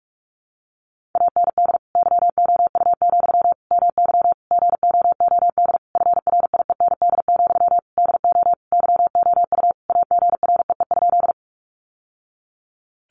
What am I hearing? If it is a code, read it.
Morse code, heard as a keyed tone, about 44 wpm, 700 Hz. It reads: AND YOU, MY GOOD FRIEND, DO YOU AGREE?
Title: Sentences from Top 1100 Words 45wpm